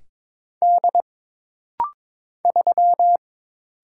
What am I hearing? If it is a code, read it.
Morse code, heard as a keyed tone, about 22 wpm, 700 Hz. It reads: D 3
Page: This rapid-fire timing is geared to push your ICR (Instant Character Recognition) skills.